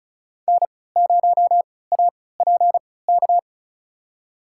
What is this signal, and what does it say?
Morse code, heard as a keyed tone, about 34 wpm, spaced as for 20 wpm, 700 Hz. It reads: N0APK
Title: USA Call Signs - 5 Characters - 3x Character Spacing - 35wpm